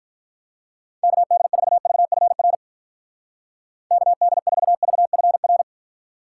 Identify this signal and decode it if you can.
Morse code, heard as a keyed tone, about 42 wpm, 700 Hz. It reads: KD4VFR KD4VFR
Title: Speed Racing - USA 2x3 Call Signs 35wpm